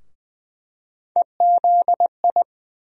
Morse code, heard as a keyed tone, about 21 wpm, 700 Hz. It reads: EZI
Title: USA Call Sign Postfixes - 3 Letters 20wpm